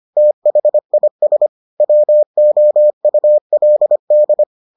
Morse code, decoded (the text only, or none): THIS WOULD